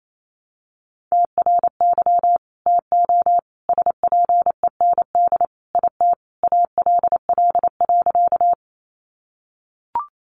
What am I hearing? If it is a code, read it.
Morse code, heard as a keyed tone, about 28 wpm, 700 Hz. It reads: TRY TO SPEND IT ALL.
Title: Sentences from Top 800 Words 28wpm